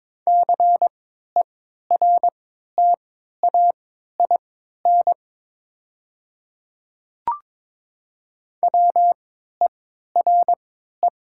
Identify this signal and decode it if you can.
Morse code, heard as a keyed tone, about 21 wpm, spaced as for 12 wpm, 700 Hz. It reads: CERTAIN WERE
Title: 3x Character Spacing - Top 500 Words - 22wpm